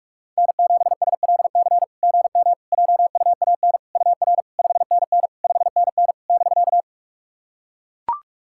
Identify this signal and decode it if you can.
Morse code, heard as a keyed tone, about 44 wpm, 690 Hz. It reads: N7SLC GM JUAN UR 5NN 5NN <BK>
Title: SOTA QSO Practice 45wpm